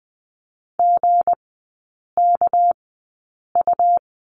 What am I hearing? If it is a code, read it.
Morse code, heard as a keyed tone, about 20 wpm, spaced as for 8 wpm, 700 Hz. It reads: GKU
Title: 198 Lesson 32e - Letter k - Three Characters 20wpm